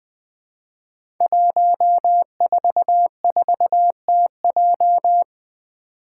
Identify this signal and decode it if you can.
Morse code, heard as a keyed tone, about 20 wpm, 700 Hz. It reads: 144TJ
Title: Five Challenging Letter-Number Combinations 20wpm